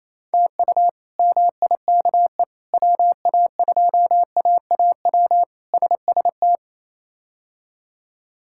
Morse code, heard as a keyed tone, about 28 wpm, 700 Hz. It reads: TU MIKE WA2AAW SST
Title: K1USN SST QSO Practice - 28wpm